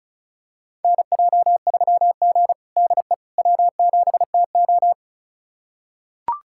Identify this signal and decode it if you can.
Morse code, heard as a keyed tone, about 36 wpm, 700 Hz. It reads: NJ3G DE W7TO
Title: USA Callsign DE Callsign - 4 Characters - 35wpm